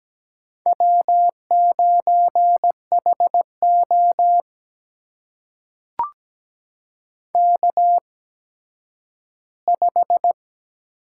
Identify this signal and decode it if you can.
Morse code, heard as a keyed tone, about 17 wpm, spaced as for 10 wpm, 700 Hz. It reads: W9HO K5
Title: USA Call Signs - 4 Characters - 8x Character Spacing - 17wpm